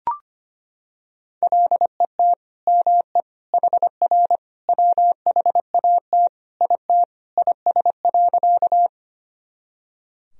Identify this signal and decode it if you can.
Morse code, heard as a keyed tone, about 25 wpm, 700 Hz. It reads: LET ME HR WHAT IT IS.
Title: Sentences from Top 300 Words and Abbreviations 25wpm